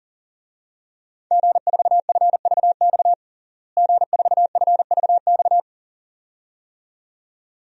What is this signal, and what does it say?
Morse code, heard as a keyed tone, about 40 wpm, 700 Hz. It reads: G4FVX G4FVX
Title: DXCC-100 Call Signs - 5 Characters - Repeated 2x - 40wpm